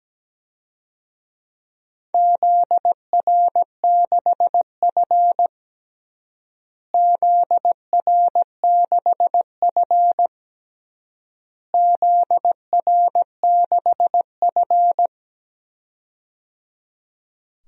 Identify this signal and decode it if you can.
Morse code, heard as a keyed tone, about 17 wpm, 700 Hz. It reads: ZR6F ZR6F ZR6F
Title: DXCC-100 Call Signs - 4 Characters - Repeated 3x - 17wpm